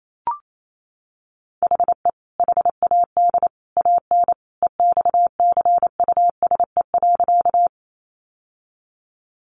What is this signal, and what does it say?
Morse code, heard as a keyed tone, about 28 wpm, 700 Hz. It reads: HE HAD AN EXCUSE.
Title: Sentences from Top 800 Words 28wpm